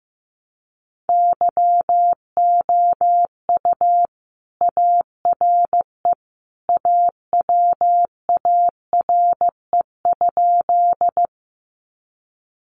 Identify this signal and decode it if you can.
Morse code, heard as a keyed tone, about 15 wpm, 700 Hz. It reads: YOU ARE AWARE?
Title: Sentences from Top 1200 Words 15wpm